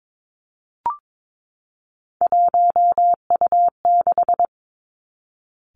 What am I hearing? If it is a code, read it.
Morse code, heard as a keyed tone, about 22 wpm, 700 Hz. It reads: 1U6
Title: Three Challenging Letter-Number Combinations 22wpm